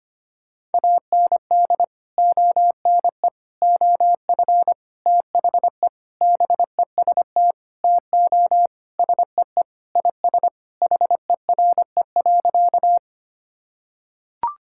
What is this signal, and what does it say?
Morse code, heard as a keyed tone, about 25 wpm, 700 Hz. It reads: AND ONE OF THE BEST TO SEE IS HERE.
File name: practice / 25wpm / Sentences with 4-Word Phrases - 25wpm